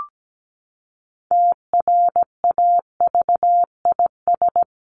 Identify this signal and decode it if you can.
Morse code, heard as a keyed tone, about 17 wpm, 700 Hz. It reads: TRAVIS
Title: A Sound Of Thunder 17wpm